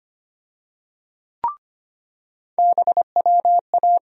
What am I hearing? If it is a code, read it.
Morse code, heard as a keyed tone, about 25 wpm, 700 Hz. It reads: BWA
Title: USA Call Sign Postfixes - 3 Letters 25wpm